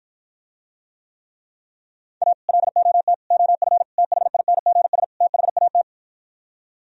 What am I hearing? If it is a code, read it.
Morse code, heard as a keyed tone, about 54 wpm, 700 Hz. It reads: A LOT OF THINGS THAT